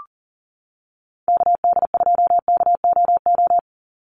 Morse code, heard as a keyed tone, about 40 wpm, 700 Hz. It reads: KD2KOO
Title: USA Call Signs 40wpm